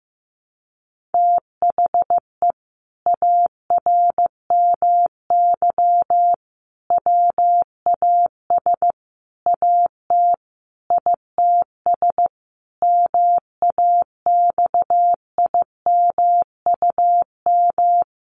Morse code, heard as a keyed tone, about 14 wpm, 700 Hz. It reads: THE ARMY WAS AT ITS MAXIMUM
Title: Sentences from Top 1300 Words 15wpm